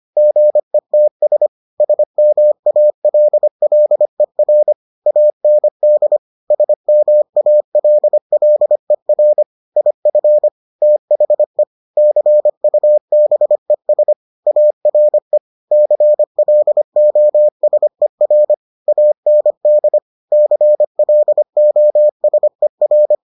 Morse code, heard as a keyed tone, about 25 wpm, 600 Hz. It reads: GETS SMALLER AND SMALLER IF THE CUBES ARE CLOSER AND CLOSER